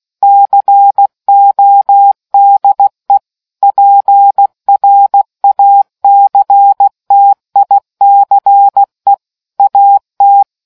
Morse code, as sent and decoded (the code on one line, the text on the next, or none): -.-. --- -.. / .--. .-. .- -.-. - .. -.-. / .- -
CODE PRACTICE AT